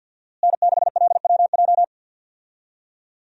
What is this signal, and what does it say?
Morse code, heard as a keyed tone, about 52 wpm, 690 Hz. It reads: N6LWJ
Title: USA Call Signs - 5 Characters - Repeated 3x - 50wpm